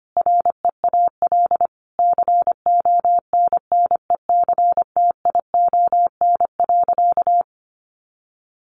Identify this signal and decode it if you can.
Morse code, heard as a keyed tone, about 25 wpm, 700 Hz. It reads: REAL CONNECTION.
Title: Sentences from Top 1300 Words 25wpm